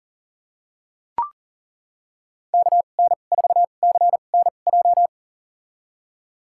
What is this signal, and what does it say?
Morse code, heard as a keyed tone, about 40 wpm, spaced as for 29 wpm, 700 Hz. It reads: KN4CNJ
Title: USA Call Signs - 6 Characters - 2x Character Spacing - 40wpm